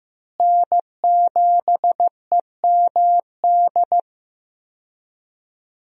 Morse code, heard as a keyed tone, about 15 wpm, 700 Hz. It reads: N7EMD
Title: USA Call Signs - 5 Characters - Repeated 3x - 15wpm